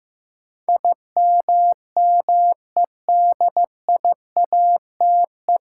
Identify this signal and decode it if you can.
Morse code, heard as a keyed tone, about 15 wpm, 700 Hz. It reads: IMMEDIATE
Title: Sentences from Top 1200 Words - New Word Drill Alt - 15wpm